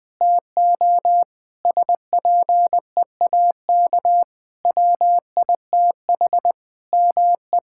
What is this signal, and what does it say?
Morse code, heard as a keyed tone, about 21 wpm, 700 Hz. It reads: TO SPEAK WITH ME
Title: Sentences from Top 1600 Words 20wpm